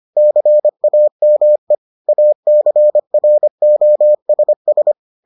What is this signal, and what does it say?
Morse code, heard as a keyed tone, about 25 wpm, 600 Hz. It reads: CAME ACROSS